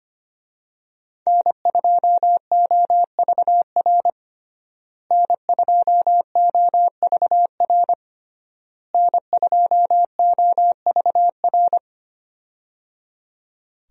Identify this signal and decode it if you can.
Morse code, heard as a keyed tone, about 25 wpm, 700 Hz. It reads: N2OVR N2OVR N2OVR
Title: USA Call Signs - 5 Characters - Repeated 3x - 25wpm